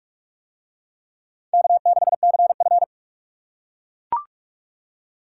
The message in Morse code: K6CF

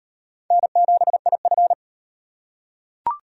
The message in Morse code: N7IF

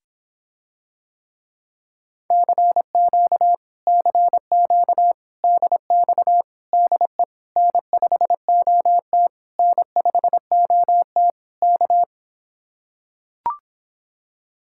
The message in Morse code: CQ CQ DX DE N5OT N5OT K

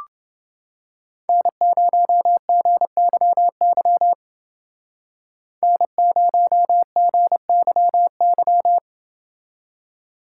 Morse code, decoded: N0GYY N0GYY

